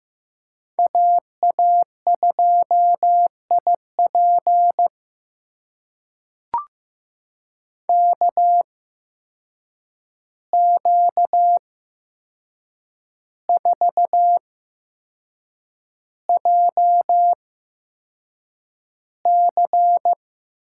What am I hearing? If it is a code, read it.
Morse code, heard as a keyed tone, about 15 wpm, spaced as for 6 wpm, 700 Hz. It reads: AA2IP KQ4JC
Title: USA Call Signs - 5 Characters - 8x Character Spacing - 15wpm